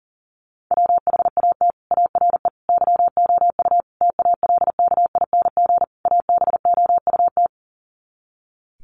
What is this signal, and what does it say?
Morse code, heard as a keyed tone, about 40 wpm, 700 Hz. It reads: WHAT ARE YOU TALKING ABOUT
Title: Sets of 5 Words 40wpm